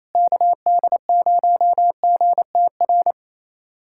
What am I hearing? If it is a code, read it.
Morse code, heard as a keyed tone, about 28 wpm, 700 Hz. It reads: KD0GTR
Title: USA Callsign DE Callsign - 6 Characters - 28wpm